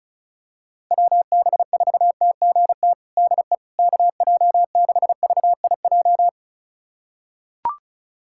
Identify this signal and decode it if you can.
Morse code, heard as a keyed tone, about 36 wpm, 700 Hz. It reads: WB4TGT DE KJ6VIJ
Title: USA Callsign DE Callsign - 6 Characters - 35wpm